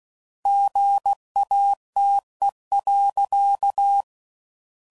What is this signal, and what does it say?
Morse code, heard as a keyed tone, about 16 wpm, 800 Hz. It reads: GATE.
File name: CW Code Groups @15WPM / Technical Subjects / OR Gate